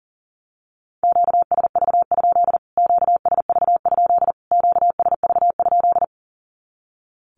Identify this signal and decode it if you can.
Morse code, heard as a keyed tone, about 40 wpm, 700 Hz. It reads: QSV? QSV? QSV?
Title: Q-Codes Repeated 3x 40wpm